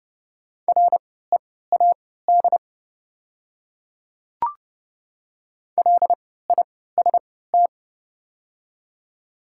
Morse code, heard as a keyed tone, about 31 wpm, spaced as for 17 wpm, 700 Hz. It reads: READ LIST
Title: 3x Character Spacing - Top 500 Words - 30wpm